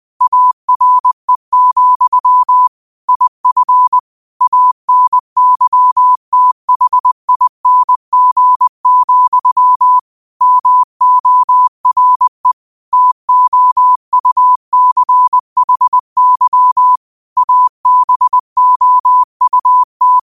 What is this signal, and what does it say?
Morse code, heard as a keyed tone, about 20 wpm, 1000 Hz. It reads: ARE, IF ANYTHING, MORE TOUCHY ABOUT
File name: qotdfiles / 20wpm